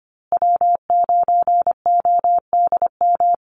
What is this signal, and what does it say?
Morse code, heard as a keyed tone, about 25 wpm, 700 Hz. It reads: W9ODM
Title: USA Call Signs Repeated 3x 25wpm